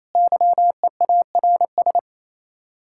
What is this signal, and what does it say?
Morse code, heard as a keyed tone, about 33 wpm, 700 Hz. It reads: YEARS
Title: Speed-Racing Common QSO Phrases - 28wpm